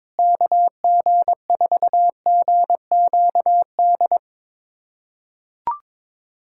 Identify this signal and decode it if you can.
Morse code, heard as a keyed tone, about 22 wpm, 700 Hz. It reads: KG4GQD
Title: USA Callsign DE Callsign - 6 Characters - 22wpm